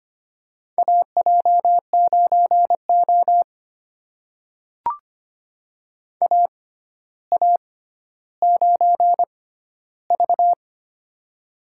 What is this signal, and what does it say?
Morse code, heard as a keyed tone, about 25 wpm, spaced as for 13 wpm, 700 Hz. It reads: AJ9O AA9V
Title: USA Call Signs - 4 Characters - 6x Character Spacing - 25wpm